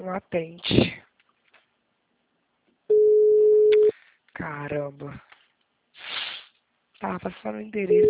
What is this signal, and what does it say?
Ringback tone (Europe, 425 Hz), with speech.